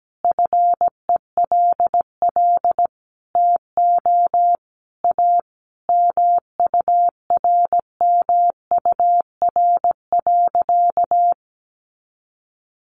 Morse code, heard as a keyed tone, about 17 wpm, 700 Hz. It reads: FELL TO A MURMUR.
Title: A Sound Of Thunder 17wpm